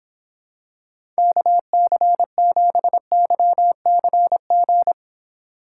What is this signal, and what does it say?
Morse code, heard as a keyed tone, about 25 wpm, 700 Hz. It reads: KC7YCG